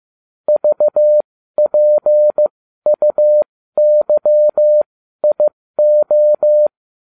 Morse code, heard as a keyed tone, about 15 wpm, spaced as for 12 wpm, 600 Hz. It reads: VPUYIO